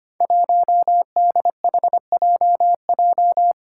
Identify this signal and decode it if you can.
Morse code, heard as a keyed tone, about 25 wpm, 700 Hz. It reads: 1DHJJ
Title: Five Challenging Letter-Number Combinations 25wpm